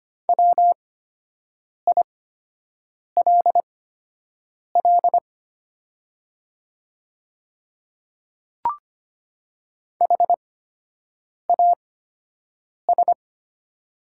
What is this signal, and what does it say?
Morse code, heard as a keyed tone, about 25 wpm, spaced as for 7 wpm, 700 Hz. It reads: WILL HAS
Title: 8x Character Spacing - Top 100 Words - 25wpm